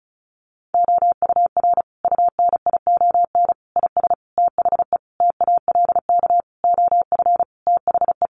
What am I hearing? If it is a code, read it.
Morse code, heard as a keyed tone, about 34 wpm, 700 Hz. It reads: OUR UNION IS THE TALK OF THE
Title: Sentences from Top 1400 Words 35wpm